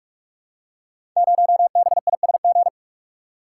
Morse code, heard as a keyed tone, about 44 wpm, 700 Hz. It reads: 0BISG
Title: Code Groups - 5 Letter:Numbers - 45wpm